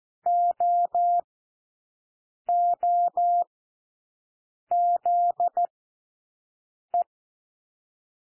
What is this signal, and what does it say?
Morse code, heard as a keyed tone, about 14 wpm, spaced as for 6 wpm, 700 Hz. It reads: OOZE
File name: cwmp3 / W1 SOUND WORDS 14 wpm 0